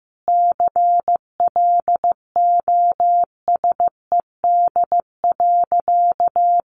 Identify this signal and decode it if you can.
Morse code, heard as a keyed tone, about 15 wpm, 700 Hz. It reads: CLOSED.
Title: Sentences from Top 1400 Words 15wpm